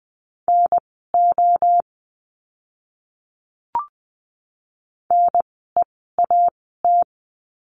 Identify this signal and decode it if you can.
Morse code, heard as a keyed tone, about 20 wpm, spaced as for 14 wpm, 700 Hz. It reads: NO NEAT
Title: 036 Lesson 7f - Letter s - Words 20wpm